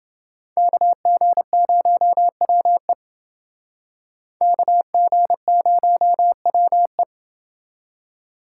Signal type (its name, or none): Morse code